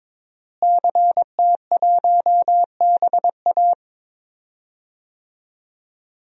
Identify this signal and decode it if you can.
Morse code, heard as a keyed tone, about 22 wpm, 700 Hz. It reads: CT1BA
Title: DXCC-100 Call Signs - 5 Characters - Repeated 3x - 22wpm